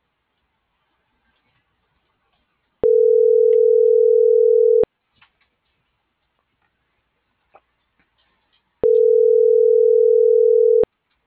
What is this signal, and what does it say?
Ringback tone (North America), 440 + 480 Hz.